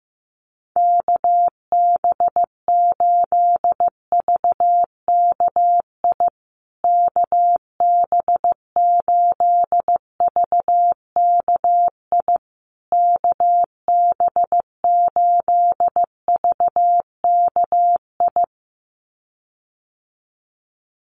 Morse code, heard as a keyed tone, about 15 wpm, 700 Hz. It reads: KB8VKI KB8VKI KB8VKI
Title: USA Call Signs Repeated 3x 15wpm